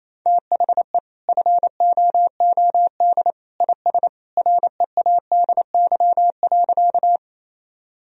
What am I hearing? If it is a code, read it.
Morse code, heard as a keyed tone, about 28 wpm, 700 Hz. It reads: THE FOOD IS READY.
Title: Sentences from Top 1000 Words - Review Part 1 - 28wpm